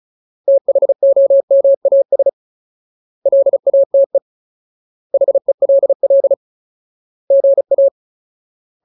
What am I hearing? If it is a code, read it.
Morse code, heard as a keyed tone, about 34 wpm, 550 Hz. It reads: THOMAS LATE HELL GA
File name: Ultimate Word List / 35WPM Ultimate